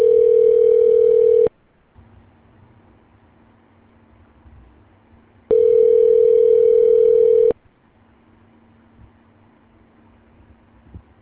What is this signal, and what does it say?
Ringback tone (North America), 440 + 480 Hz.